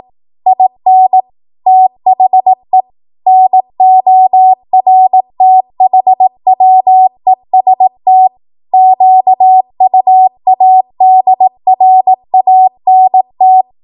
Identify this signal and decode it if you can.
Morse code, heard as a keyed tone, about 18 wpm, 750 Hz. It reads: IN THE NORTHWEST QUADRANT
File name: Morse / 18WPM